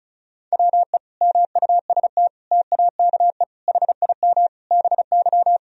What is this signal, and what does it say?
Morse code, heard as a keyed tone, about 34 wpm, 700 Hz. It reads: WE MUST TAKE HIM BY
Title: Sentences from Top 1000 Words - Review Part 1 - 35wpm